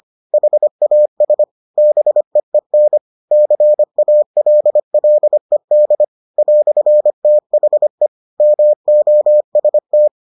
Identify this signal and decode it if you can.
Morse code, heard as a keyed tone, about 25 wpm, 600 Hz. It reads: HAS BEEN CALLED "THE MOST